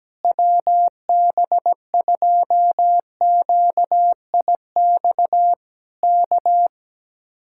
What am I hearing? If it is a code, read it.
Morse code, heard as a keyed tone, about 17 wpm, 700 Hz. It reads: WB2QIX K
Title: Speed-Racing Common QSO Phrases - 17wpm